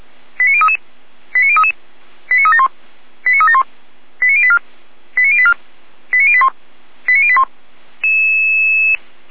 Five-tone selective call (ZVEI), heard as quick runs of five tones, tones of about 70 ms. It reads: 89033 89033 89371 89371 89084 89084 89081 89081